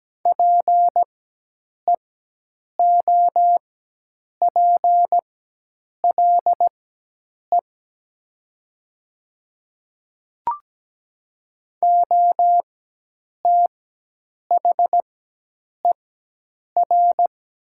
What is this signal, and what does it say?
Morse code, heard as a keyed tone, about 17 wpm, spaced as for 8 wpm, 700 Hz. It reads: PEOPLE OTHER
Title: 4x Character Spacing - Top 100 Words - 17wpm